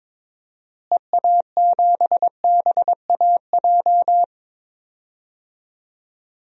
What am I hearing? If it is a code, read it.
Morse code, heard as a keyed tone, about 22 wpm, 700 Hz. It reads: EA7BAJ
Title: DXCC-100 Call Signs - 6 Characters - Repeated 3x - 22wpm